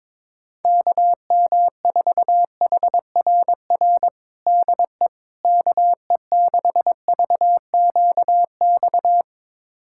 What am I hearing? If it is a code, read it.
Morse code, heard as a keyed tone, about 22 wpm, 700 Hz. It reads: KM4HRR DE KE6VQX
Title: USA Callsign DE Callsign 22wpm